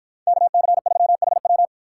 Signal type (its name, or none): Morse code